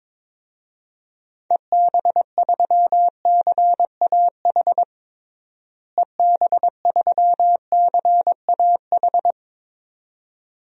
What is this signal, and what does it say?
Morse code, heard as a keyed tone, about 22 wpm, 700 Hz. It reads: EB3CAH EB3CAH
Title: DXCC-100 Call Signs - 6 Characters - Repeated 2x - 22wpm